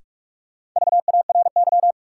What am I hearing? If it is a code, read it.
Morse code, heard as a keyed tone, about 43 wpm, 700 Hz. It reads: VARY
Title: Sentences from Top 1200 Words - New Word Drill - 45wpm